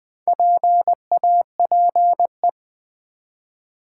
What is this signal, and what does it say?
Morse code, heard as a keyed tone, about 20 wpm, 700 Hz. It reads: PAPE
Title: Build A Word - 1 Letter At A Time - 5-Letter Words - 20wpm